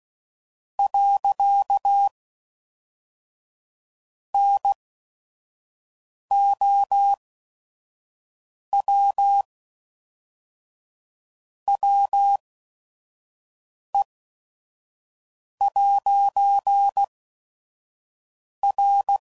Morse code, heard as a keyed tone, about 16 wpm, spaced as for 5 wpm, 800 Hz. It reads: . NOW WE'R
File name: CW Code Groups @5WPM / Technical Subjects / AND Gate